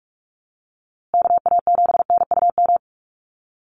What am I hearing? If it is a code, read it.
Morse code, heard as a keyed tone, about 44 wpm, 700 Hz. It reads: KA7NUM
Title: USA Call Signs Repeated 2x 45wpm